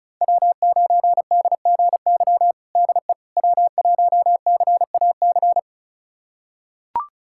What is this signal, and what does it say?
Morse code, heard as a keyed tone, about 36 wpm, 700 Hz. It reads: W9DGY DE W1CAC
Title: USA Callsign DE Callsign - 5 Characters - 35wpm